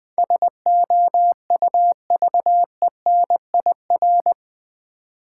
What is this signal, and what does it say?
Morse code, heard as a keyed tone, about 20 wpm, 690 Hz. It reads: SOUVENIR
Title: Difficult Words with Hints - 20wpm